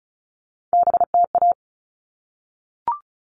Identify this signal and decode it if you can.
Morse code, heard as a keyed tone, about 34 wpm, 700 Hz. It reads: BTA
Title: All Cut Numbers - 3-Digit 35wpm